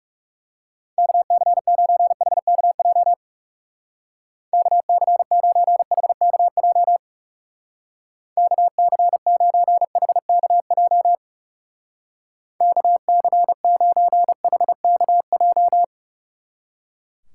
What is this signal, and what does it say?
Morse code, heard as a keyed tone, about 39 wpm, 690 Hz. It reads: KC9HKJ KC9HKJ KC9HKJ KC9HKJ